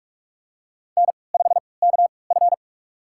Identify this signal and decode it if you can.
Morse code, heard as a keyed tone, about 46 wpm, spaced as for 25 wpm, 700 Hz. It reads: N5KF
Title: USA Call Signs - 4 Characters - 3x Character Spacing - 45wpm